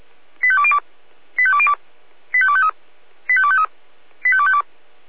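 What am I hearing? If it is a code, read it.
Five-tone selective call (ZVEI), heard as quick runs of five tones, tones of about 70 ms. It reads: 85202 85202 85253 85253 85252
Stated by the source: Received on SDR, ZVEI five-tone call 85102.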